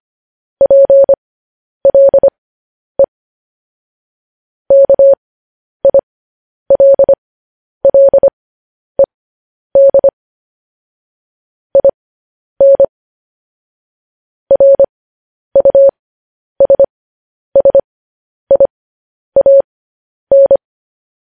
Morse code, heard as a keyed tone, about 25 wpm, spaced as for 10 wpm, 550 Hz. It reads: PLE KILLED IN RUSSIAN
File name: podcast / News headlines-1x-25x10